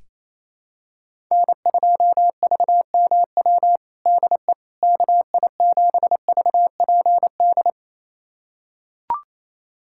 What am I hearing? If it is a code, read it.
Morse code, heard as a keyed tone, about 28 wpm, 700 Hz. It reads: N2VMW DE KI7VPD
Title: USA Callsign DE Callsign 28wpm